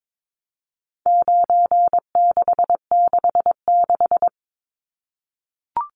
Morse code, heard as a keyed tone, about 22 wpm, 700 Hz. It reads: 9666
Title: Numbers - 4-Digit 22wpm